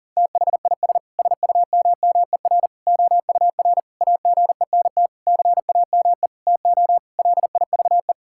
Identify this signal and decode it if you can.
Morse code, heard as a keyed tone, about 40 wpm, 700 Hz. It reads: THIS SUMMER OUR AGENT CAME TO LIVE
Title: Sentences from Top 1600 Words 40wpm